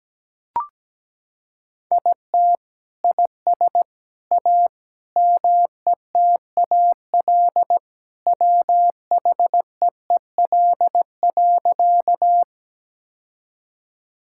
Morse code, heard as a keyed tone, about 17 wpm, 700 Hz. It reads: IT IS A METAL WHEEL.
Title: Sentences from Top 1000 Words - Review Part 6 - 17wpm